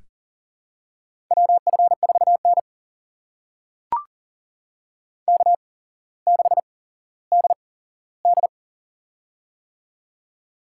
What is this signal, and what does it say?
Morse code, heard as a keyed tone, about 40 wpm, spaced as for 17 wpm, 700 Hz. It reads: WF4N K6DD
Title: USA Call Signs - 4 Characters - 8x Character Spacing - 40wpm